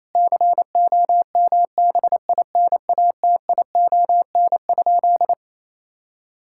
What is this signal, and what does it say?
Morse code, heard as a keyed tone, about 27 wpm, 700 Hz. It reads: COMBINATION?
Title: Sentences from Top 1000 Words - Review Part 6 - 28wpm